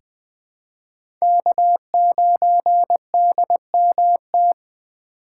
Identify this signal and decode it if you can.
Morse code, heard as a keyed tone, about 20 wpm, 700 Hz. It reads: K9DMT
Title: USA Call Signs - 5 Characters - 4x Character Spacing - 20wpm